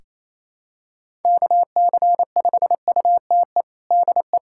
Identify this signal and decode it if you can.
Morse code, heard as a keyed tone, about 28 wpm, 700 Hz. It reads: KC5UTE DE
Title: USA Callsign DE Callsign 28wpm